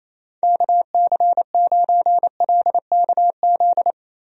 Morse code, heard as a keyed tone, about 27 wpm, 700 Hz. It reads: KC9LKZ
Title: USA Call Signs - 6 Characters - Repeated 2x - 28wpm